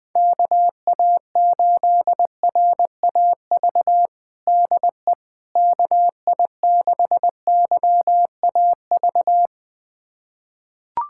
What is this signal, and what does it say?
Morse code, heard as a keyed tone, about 21 wpm, 700 Hz. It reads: KA8RAV DE KI6YAV
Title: USA Callsign DE Callsign - 6 Characters - 20wpm